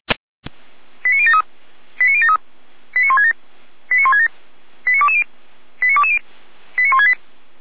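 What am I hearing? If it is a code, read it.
Five-tone selective call (ZVEI), heard as quick runs of five tones, tones of about 70 ms. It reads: 89073 89073 89167 89167 89220 89220 89168
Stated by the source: ZVEI five-tone call received on an SDR.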